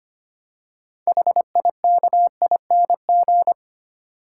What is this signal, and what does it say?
Morse code, heard as a keyed tone, about 30 wpm, 700 Hz. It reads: HIKING
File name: Easy Words - Speed-Racing with Hints - 25wpm